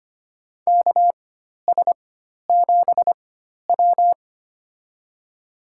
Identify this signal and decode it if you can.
Morse code, heard as a keyed tone, about 25 wpm, spaced as for 12 wpm, 700 Hz. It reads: KS7W
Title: USA Call Signs - 4x Character Spacing - 25wpm